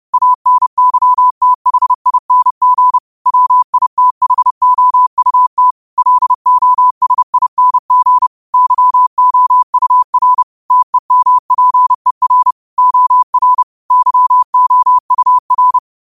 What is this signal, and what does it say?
Morse code, heard as a keyed tone, about 30 wpm, 1000 Hz. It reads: ANYTHING WITHOUT LOSING YOUR TEMPER OR YOUR
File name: qotdfiles / 30wpm